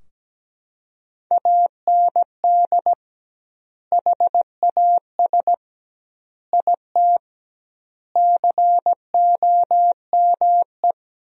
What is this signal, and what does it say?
Morse code, heard as a keyed tone, about 17 wpm, 700 Hz. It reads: AND HAS IT COME
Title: 2x Word-Spacing - Sentences from Top 100 Words - 17wpm